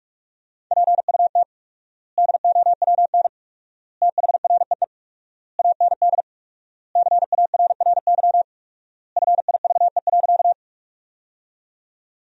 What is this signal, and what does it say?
Morse code, heard as a keyed tone, about 46 wpm, 700 Hz. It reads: PUT DOWN THREE AND CARRY FIVE.
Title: 4x Word-Spacing - Sentences from Top 400 Words - 45wpm